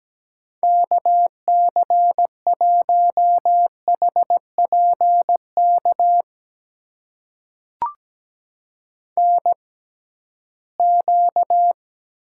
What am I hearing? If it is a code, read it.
Morse code, heard as a keyed tone, about 17 wpm, spaced as for 8 wpm, 700 Hz. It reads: KC1HP KNQ
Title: USA Call Signs - 6x Character Spacing - 17wpm